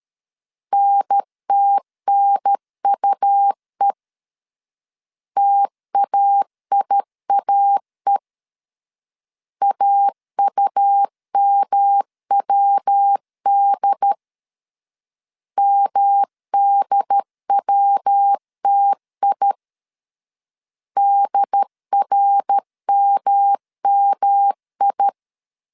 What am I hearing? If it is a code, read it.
Morse code, heard as a keyed tone, about 13 wpm, 800 Hz. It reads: NTNUE TAIAE AUMWD MDWTI DRMMI
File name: CW / DET / 1 3 1500C S C 12wpm 2750